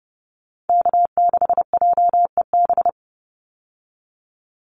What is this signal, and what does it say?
Morse code, heard as a keyed tone, about 30 wpm, 700 Hz. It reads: K6JEB
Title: Call Signs Repeated 3x - 30wpm